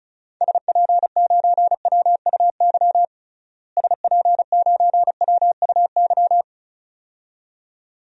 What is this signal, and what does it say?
Morse code, heard as a keyed tone, about 34 wpm, 700 Hz. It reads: SP9WUY SP9WUY
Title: DXCC-100 Call Signs - 6 Characters - Repeated 2x - 35wpm